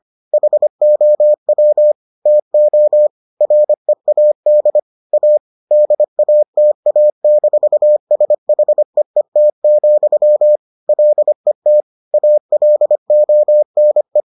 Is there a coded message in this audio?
Yes: Morse code from 0.3 to 14.2 s, heard as a keyed tone, about 25 wpm, 600 Hz. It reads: HOW TO READ A DATA-SHEET, LET ALONE